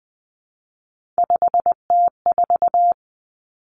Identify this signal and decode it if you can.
Morse code, heard as a keyed tone, about 20 wpm, 700 Hz. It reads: 5T4
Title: Three Challenging Letter-Number Combinations 20wpm